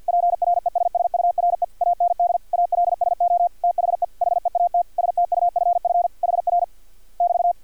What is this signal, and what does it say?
Morse code, heard as a keyed tone, about 50 wpm, 700 Hz. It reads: PRESSURE AND ALSO THE HEAT STUFF HR <BT>
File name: archived / CRSnet